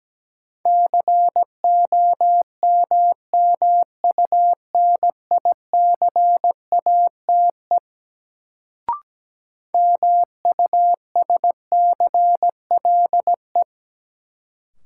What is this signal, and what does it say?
Morse code, heard as a keyed tone, about 17 wpm, 700 Hz. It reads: COMMUNICATE MUSCLE
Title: Sentences from Top 1300 Words - New Word Drill - 17wpm